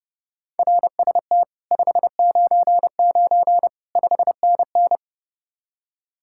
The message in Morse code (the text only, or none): RST 599 5NN